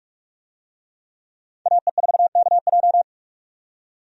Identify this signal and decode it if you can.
Morse code, heard as a keyed tone, about 44 wpm, 700 Hz. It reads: AE4KJ
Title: USA Call Signs - 5 Characters - 45wpm